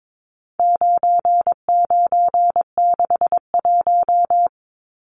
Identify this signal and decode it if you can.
Morse code, heard as a keyed tone, about 22 wpm, 700 Hz. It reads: 9961
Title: Numbers - 4-Digit 22wpm